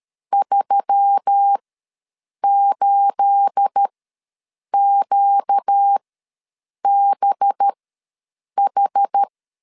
Morse code, heard as a keyed tone, about 13 wpm, spaced as for 7 wpm, 800 Hz. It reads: 38QBH
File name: LEZIONI / 9 28 ws 12wpm 3354